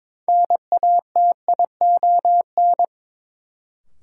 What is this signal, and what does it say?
Morse code, heard as a keyed tone, about 22 wpm, 700 Hz. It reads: NATION
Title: Sets of 3 Words - New Word Drill - 22wpm